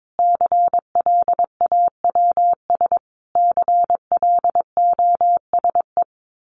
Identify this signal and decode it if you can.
Morse code, heard as a keyed tone, about 22 wpm, 700 Hz. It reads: CLAWS CLOSE
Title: A Sound Of Thunder 22wpm